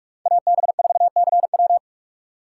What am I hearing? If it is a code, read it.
Morse code, heard as a keyed tone, about 46 wpm, 700 Hz. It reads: AB4CW
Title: USA Call Signs - 5 Characters - 3x Character Spacing - 45wpm